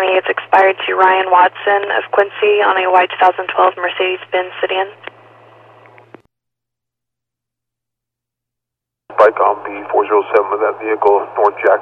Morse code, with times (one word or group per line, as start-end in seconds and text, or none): none